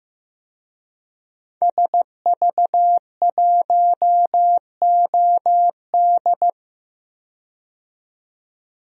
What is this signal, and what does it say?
Morse code, heard as a keyed tone, about 15 wpm, 700 Hz. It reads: SV1OD
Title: DXCC-100 Call Signs - 5 Characters - 15wpm